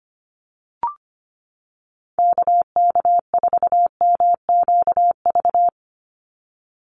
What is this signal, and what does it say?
Morse code, heard as a keyed tone, about 25 wpm, 700 Hz. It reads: KK4MQV
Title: USA Call Signs 25wpm